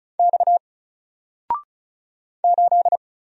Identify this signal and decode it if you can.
Morse code, heard as a keyed tone, about 34 wpm, 700 Hz. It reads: X 8
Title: Single Character (Letter:Number:Punctuation:Prosign) Alt 35wpm